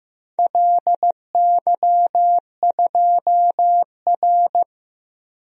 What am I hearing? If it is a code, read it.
Morse code, heard as a keyed tone, about 15 wpm, 700 Hz. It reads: LY2R
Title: DXCC-100 Call Signs - 4 Characters - Repeated 2x - 15wpm